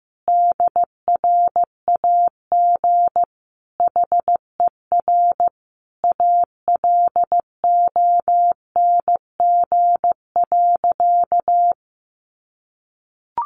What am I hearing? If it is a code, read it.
Morse code, heard as a keyed tone, about 15 wpm, 700 Hz. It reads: DRAG HER ALONG.
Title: Sentences from Top 1500 Words 15wpm